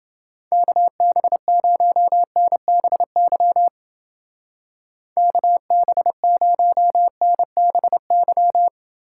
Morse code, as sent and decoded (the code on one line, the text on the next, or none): -.- -... ----- -. -... -.-- / -.- -... ----- -. -... -.--
KB0NBY KB0NBY